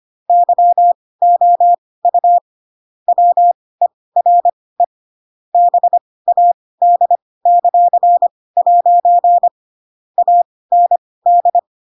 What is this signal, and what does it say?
Morse code, heard as a keyed tone, about 25 wpm, spaced as for 18 wpm, 700 Hz. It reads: YOU WERE BAD;' AND
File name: Pinocchio / Pinocchio ch7 12 25 ik1hge